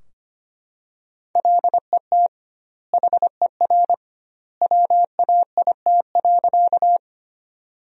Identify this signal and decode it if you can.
Morse code, heard as a keyed tone, about 25 wpm, 700 Hz. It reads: LET HER WAIT.
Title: 2x Word-Spacing - Sentences from Top 500 Words - 25wpm